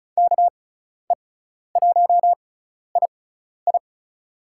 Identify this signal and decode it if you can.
Morse code, heard as a keyed tone, about 34 wpm, spaced as for 12 wpm, 700 Hz. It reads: KE1II
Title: USA Call Signs - 6x Character Spacing - 35wpm